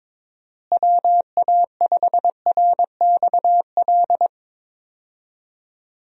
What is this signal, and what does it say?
Morse code, heard as a keyed tone, about 22 wpm, 700 Hz. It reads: WA5RXL
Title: USA Call Signs - 6 Characters - Repeated 3x - 22wpm